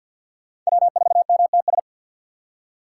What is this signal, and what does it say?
Morse code, heard as a keyed tone, about 55 wpm, 700 Hz. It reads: W4MTS
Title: Speed Racing - USA 1x3 Call Signs 50wpm